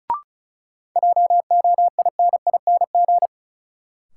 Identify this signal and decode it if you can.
Morse code, heard as a keyed tone, about 36 wpm, 700 Hz. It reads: JOINING
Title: Sets of 5 Words - New Word Drill - 35wpm